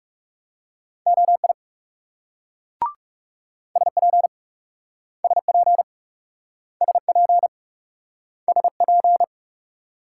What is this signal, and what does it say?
Morse code, heard as a keyed tone, about 37 wpm, 700 Hz. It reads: OI SP SP SP SP
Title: Speed-Racing Multi-letter Phonemes - 30wpm